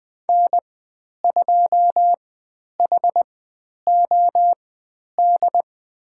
Morse code, heard as a keyed tone, about 20 wpm, spaced as for 10 wpm, 700 Hz. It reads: N2HOD
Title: 102 Lesson 17g - Letter u - Call Signs 20wpm